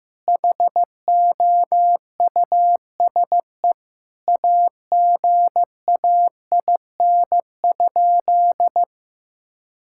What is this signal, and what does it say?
Morse code, heard as a keyed tone, about 15 wpm, 700 Hz. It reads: HOUSE AGAIN?
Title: Sentences from Top 1000 Words - Review Part 6 - 15wpm